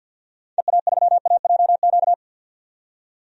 Morse code, heard as a keyed tone, about 53 wpm, 700 Hz. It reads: EA3AJQ
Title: DXCC-100 Call Signs - 6 Characters - 50wpm